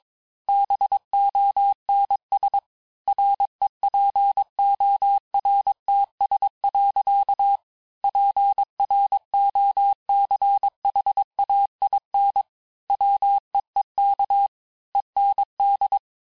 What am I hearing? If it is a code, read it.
Morse code, heard as a keyed tone, about 22 wpm, 800 Hz. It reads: BONS REPORTS. PROCHAIN WEEK END
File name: cours-du-mercredi / audios / cours-854-liste-fin-QSO-vitesse-21-mpm